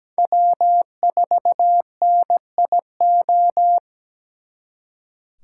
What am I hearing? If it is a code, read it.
Morse code, heard as a keyed tone, about 20 wpm, 700 Hz. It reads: W4NIO
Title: Speed Racing - USA Call Signs 17wpm